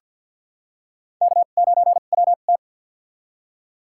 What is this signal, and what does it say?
Morse code, heard as a keyed tone, about 46 wpm, spaced as for 35 wpm, 700 Hz. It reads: K9WT
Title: USA Call Signs - 4 Characters - 2x Character Spacing - 50wpm